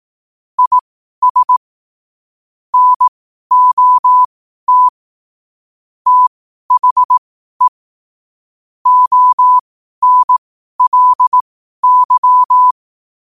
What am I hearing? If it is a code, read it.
Morse code, heard as a keyed tone, about 18 wpm, spaced as for 12 wpm, 1000 Hz. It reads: IS NOT THE ONLY
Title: 12 WPM morse code quotes for Thu, 29 Feb 2024